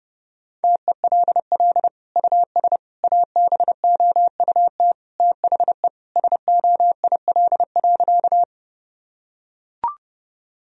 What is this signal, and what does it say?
Morse code, heard as a keyed tone, about 30 wpm, 700 Hz. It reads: TELL US ABOUT THE SOIL.
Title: Sentences from Top 1000 Words - Review Part 4 - 30wpm